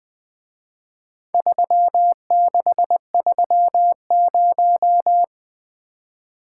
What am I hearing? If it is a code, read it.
Morse code, heard as a keyed tone, about 23 wpm, 700 Hz. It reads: 3630
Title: Speed-Racing - Four Letter-Number Combinations - 20wpm